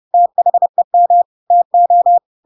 Morse code, heard as a keyed tone, about 30 wpm, 700 Hz. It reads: THEM TO
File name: Pinocchio / Pinocchio ch7 30 30 ik1hge